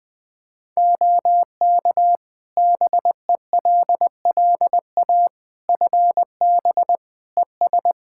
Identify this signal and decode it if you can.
Morse code, heard as a keyed tone, about 20 wpm, 700 Hz. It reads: OK BELLA FB ES
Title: K1USN SST QSO Practice 20wpm